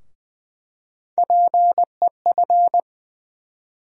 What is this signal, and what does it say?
Morse code, heard as a keyed tone, about 20 wpm, 700 Hz. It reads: PEF
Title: USA Call Sign Postfixes - 3 Letters 20wpm